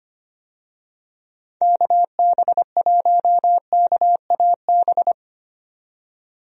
Morse code, heard as a keyed tone, about 25 wpm, 700 Hz. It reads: KB1KAB
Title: USA Call Signs - 6 Characters - 25wpm